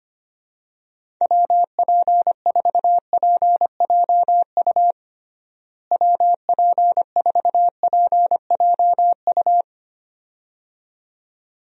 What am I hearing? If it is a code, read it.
Morse code, heard as a keyed tone, about 25 wpm, 700 Hz. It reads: WP4PJU WP4PJU
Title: USA Call Signs - 6 Characters - Repeated 2x - 25wpm